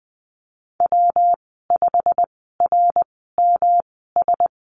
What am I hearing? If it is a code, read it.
Morse code, heard as a keyed tone, about 20 wpm, spaced as for 15 wpm, 700 Hz. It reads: W5RMS
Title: 122 Lesson 20g - Number 3 - Call Signs 20wpm